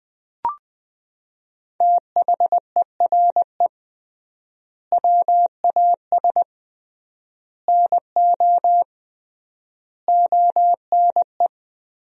Morse code, heard as a keyed tone, about 20 wpm, 700 Hz. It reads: THERE WAS NO ONE
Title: 3x Word-Spacing - Sentences from Top 200 Words - 20wpm